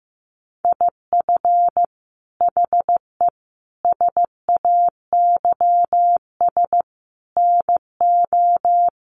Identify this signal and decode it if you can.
Morse code, heard as a keyed tone, about 15 wpm, 700 Hz. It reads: IF HE SAYS NO
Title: A Sound Of Thunder 15wpm